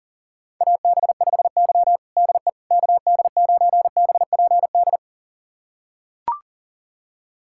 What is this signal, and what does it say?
Morse code, heard as a keyed tone, about 40 wpm, 700 Hz. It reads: AB5Y DE KD9BPD
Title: USA Callsign DE Callsign 40wpm